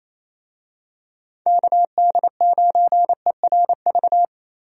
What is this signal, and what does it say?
Morse code, heard as a keyed tone, about 28 wpm, 700 Hz. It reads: KD9ERV
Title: USA Call Signs - 6 Characters - Repeated 3x - 28wpm